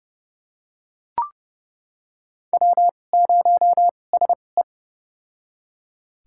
Morse code, heard as a keyed tone, about 30 wpm, spaced as for 22 wpm, 700 Hz. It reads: W0SE
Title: USA Call Signs - 2x Character Spacing - 30wpm